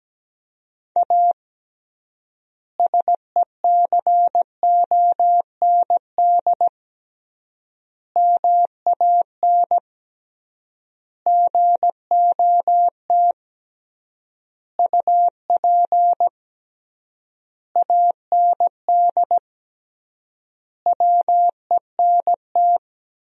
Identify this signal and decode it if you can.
Morse code, heard as a keyed tone, about 17 wpm, 700 Hz. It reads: A SECOND MAN GOT UP AND WENT
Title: 3x Word-Spacing - Sentences from Top 300 Words - 17wpm